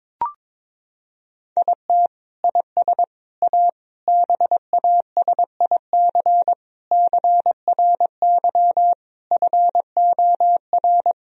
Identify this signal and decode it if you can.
Morse code, heard as a keyed tone, about 22 wpm, 700 Hz. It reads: IT IS A BASIC CRY FOR
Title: Sentences from Top 1000 Words - Review Part 3 - 22wpm